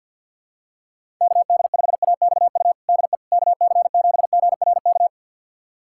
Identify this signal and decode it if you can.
Morse code, heard as a keyed tone, about 51 wpm, 700 Hz. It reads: KD5AXU DE KC7GRK
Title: USA Callsign DE Callsign - 6 Characters - 50wpm